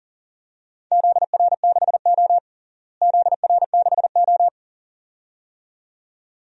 Morse code, heard as a keyed tone, about 40 wpm, 700 Hz. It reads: ZR6O ZR6O
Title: DXCC-100 Call Signs - 4 Characters - Repeated 2x - 40wpm